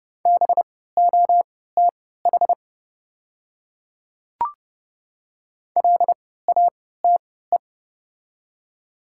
Morse code, heard as a keyed tone, about 30 wpm, spaced as for 17 wpm, 700 Hz. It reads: BOTH LATE
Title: 3x Character Spacing - Top 500 Words - 30wpm